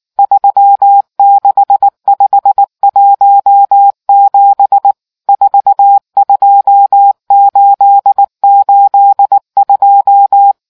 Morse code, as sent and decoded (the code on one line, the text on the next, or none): ...-- -.... ..... .---- --... / ....- ..--- ---.. ---.. ..---
36517 42882